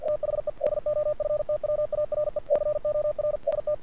Morse code, heard as a keyed tone, about 48 wpm, 590 Hz. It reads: THE DOFTWARE COR ST
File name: archived / CRSnet